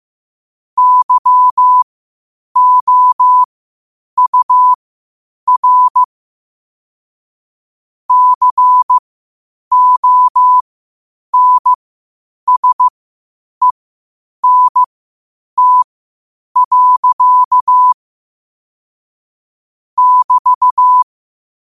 Morse code, heard as a keyed tone, about 15 wpm, spaced as for 8 wpm, 1000 Hz. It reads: YOUR CONSENT. <BT>